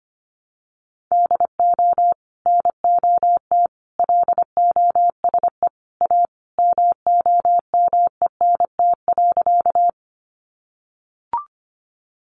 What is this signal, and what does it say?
Morse code, heard as a keyed tone, about 25 wpm, 700 Hz. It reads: DO NOT LOSE A MOMENT.
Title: Sentences from Top 1100 Words 25wpm